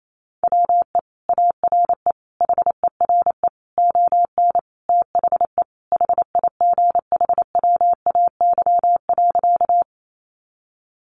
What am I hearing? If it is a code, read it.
Morse code, heard as a keyed tone, about 28 wpm, 700 Hz. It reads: WE ARE HERE ON THE HIGHWAY.
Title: Sentences from Top 1400 Words 28wpm